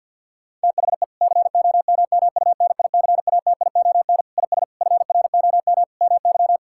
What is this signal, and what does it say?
Morse code, heard as a keyed tone, about 50 wpm, 700 Hz. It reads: THE COMMUNICATION IS FROM MY
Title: Sentences from Top 1000 Words - Review Part 6 - 50wpm